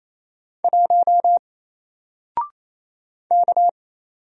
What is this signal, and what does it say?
Morse code, heard as a keyed tone, about 28 wpm, 700 Hz. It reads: 1 K
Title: Single Letter-Number Alt 28wpm